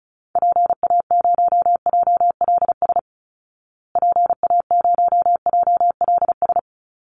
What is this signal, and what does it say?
Morse code, heard as a keyed tone, about 34 wpm, 700 Hz. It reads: PA0JLS PA0JLS
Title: Call Signs Repeated 2x - 35wpm